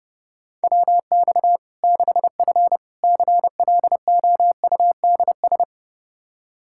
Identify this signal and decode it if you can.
Morse code, heard as a keyed tone, about 32 wpm, 700 Hz. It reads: WX 6F CLOUDS
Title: Practice QSO - Exchange 2 - Info - 30wpm